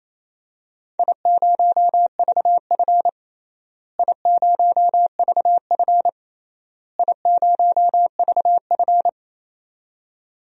Morse code, heard as a keyed tone, about 28 wpm, 700 Hz. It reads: I0VF I0VF I0VF
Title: DXCC-100 Call Signs - 4 Characters - Repeated 3x - 28wpm